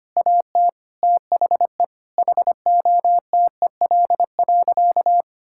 Morse code, heard as a keyed tone, about 25 wpm, 700 Hz. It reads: AT THE HOTEL.
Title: Sentences from Top 1600 Words 25wpm